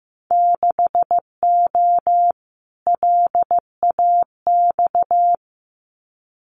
Morse code, heard as a keyed tone, about 15 wpm, 700 Hz. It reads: 6O LAX
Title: Winter Field Day Exchanges Encore 15wpm